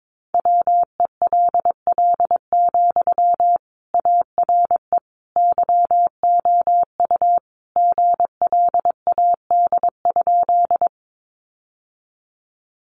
Morse code, heard as a keyed tone, about 22 wpm, 700 Hz. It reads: WELL, ARE YOU GLAD?
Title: Sentences from Top 800 Words 22wpm